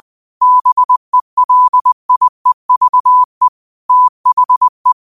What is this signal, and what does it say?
Morse code, heard as a keyed tone, about 20 wpm, 1000 Hz. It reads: BELIEVE THE
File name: qotdfiles / 20wpm